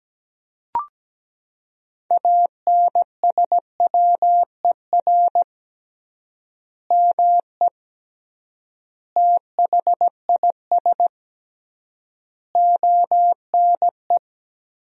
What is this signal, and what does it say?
Morse code, heard as a keyed tone, about 17 wpm, 700 Hz. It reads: ANSWER ME THIS ONE
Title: 3x Word-Spacing - Sentences from Top 400 Words - 17wpm